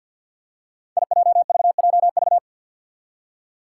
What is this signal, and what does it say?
Morse code, heard as a keyed tone, about 47 wpm, 700 Hz. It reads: IJVJV
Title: Five Challenging Letter-Number Combinations 50wpm